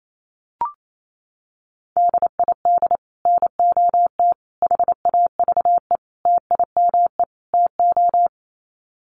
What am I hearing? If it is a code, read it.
Morse code, heard as a keyed tone, about 28 wpm, 700 Hz. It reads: DID NOT HAVE TIME TO
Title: Sets of 5 Words Encore 28wpm